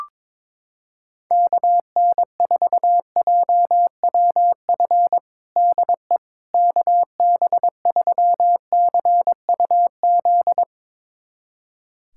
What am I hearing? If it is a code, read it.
Morse code, heard as a keyed tone, about 22 wpm, 700 Hz. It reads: KN4JWF DE KB3CUZ
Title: USA Callsign DE Callsign 22wpm